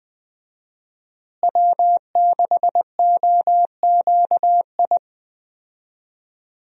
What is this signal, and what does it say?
Morse code, heard as a keyed tone, about 20 wpm, 700 Hz. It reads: W6OQI
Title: USA Call Signs - 5 Characters - Repeated 3x - 20wpm